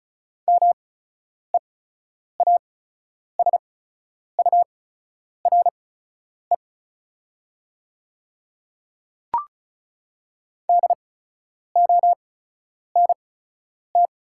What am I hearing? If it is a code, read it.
Morse code, heard as a keyed tone, about 35 wpm, spaced as for 10 wpm, 700 Hz. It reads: MEASURE DONT
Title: 8x Character Spacing - Top 500 Words - 35wpm